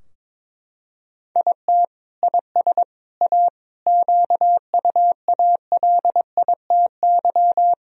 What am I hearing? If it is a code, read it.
Morse code, heard as a keyed tone, about 22 wpm, 700 Hz. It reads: IT IS A QUALITY
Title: Sentences from Top 1000 Words - Review Part 4 - 22wpm